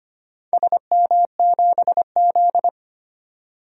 Morse code, heard as a keyed tone, about 25 wpm, 700 Hz. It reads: SM7Z
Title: DXCC-100 Call Signs - 4 Characters - Repeated 3x - 25wpm